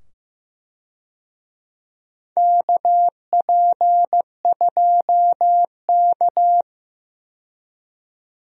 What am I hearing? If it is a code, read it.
Morse code, heard as a keyed tone, about 15 wpm, 700 Hz. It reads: KP2K
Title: USA Call Signs - 4 Characters - 15wpm